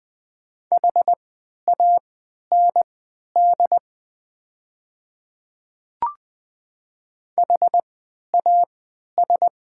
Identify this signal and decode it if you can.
Morse code, heard as a keyed tone, about 20 wpm, spaced as for 11 wpm, 690 Hz. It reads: HAND HAS
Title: 3x Character Spacing - Top 300 Words - 20wpm